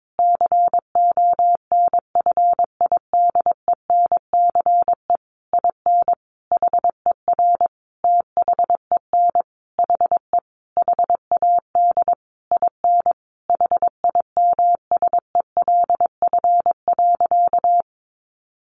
Morse code, heard as a keyed tone, about 22 wpm, 700 Hz. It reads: CONFIDENCE IN HER THEN HE HAD IN HIMSELF.
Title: Sentences from Top 1500 Words 22wpm